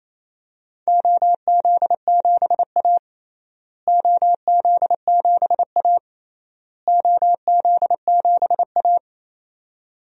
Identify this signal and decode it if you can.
Morse code, heard as a keyed tone, about 28 wpm, 700 Hz. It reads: OZ7A OZ7A OZ7A
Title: DXCC-100 Call Signs - 4 Characters - Repeated 3x - 28wpm